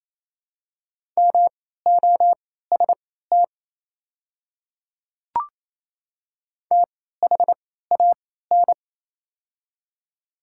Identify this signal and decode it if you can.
Morse code, heard as a keyed tone, about 28 wpm, spaced as for 16 wpm, 700 Hz. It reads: MOST THAN
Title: 3x Character Spacing - Top 100 Words - 28wpm